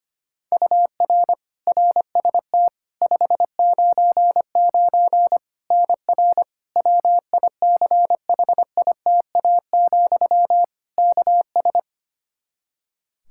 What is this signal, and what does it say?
Morse code, heard as a keyed tone, about 25 wpm, 700 Hz. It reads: UR RST 599 NR WICHITA, KS
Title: K1USN SST QSO Practice 25wpm